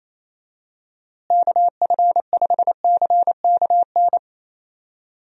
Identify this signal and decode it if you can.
Morse code, heard as a keyed tone, about 27 wpm, 700 Hz. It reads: KF5CKN
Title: USA Call Signs - 6 Characters - Repeated 2x - 28wpm